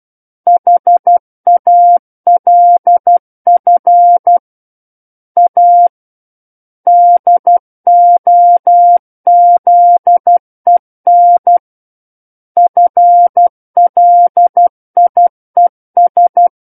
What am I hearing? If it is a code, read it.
Morse code, heard as a keyed tone, about 12 wpm, 700 Hz. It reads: HALF A DOZEN FLIES
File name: Book One / 03-Chapter 3 - 12 WPM